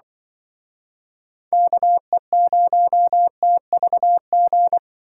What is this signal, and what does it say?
Morse code, heard as a keyed tone, about 24 wpm, 700 Hz. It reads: KE0TVG
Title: Speed Racing - USA 2x3 Call Signs 20wpm